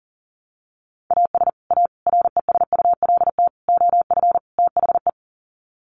Morse code, heard as a keyed tone, about 40 wpm, 700 Hz. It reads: AS A RESULT OF THE
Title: Sets of 5 Words 40wpm